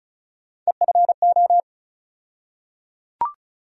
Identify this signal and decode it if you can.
Morse code, heard as a keyed tone, about 36 wpm, 700 Hz. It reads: EFO
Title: USA Call Sign Postfixes - 3 Letters 35wpm